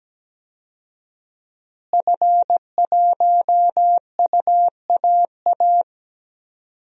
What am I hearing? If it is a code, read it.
Morse code, heard as a keyed tone, about 17 wpm, 700 Hz. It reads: F1UAA
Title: DXCC-100 Call Signs - 5 Characters - Repeated 3x - 17wpm